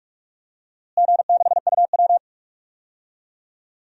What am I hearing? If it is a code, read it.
Morse code, heard as a keyed tone, about 46 wpm, 700 Hz. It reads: G6UW
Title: DXCC-100 Call Signs - 4 Characters - Repeated 2x - 45wpm